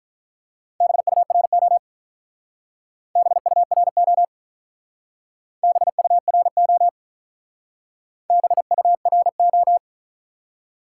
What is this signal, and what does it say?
Morse code, heard as a keyed tone, about 43 wpm, 700 Hz. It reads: BURO BURO BURO BURO
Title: Speed-Racing Top 90 QSO Elements 35wpm